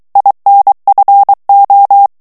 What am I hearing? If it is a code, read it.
Morse code, heard as a keyed tone, about 23 wpm, 800 Hz. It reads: INFO
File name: morse mp3 / ITALIEN texte clair 1 vitesse 22 mots minute